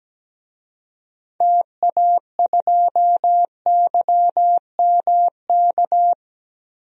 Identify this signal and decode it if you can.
Morse code, heard as a keyed tone, about 17 wpm, 700 Hz. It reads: TA2YMK
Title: DXCC-100 Call Signs - 6 Characters - 17wpm